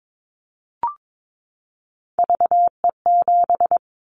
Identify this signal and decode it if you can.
Morse code, heard as a keyed tone, about 22 wpm, 700 Hz. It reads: VE7
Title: Callsign Prefixes for Top 100 DXCC - 22wpm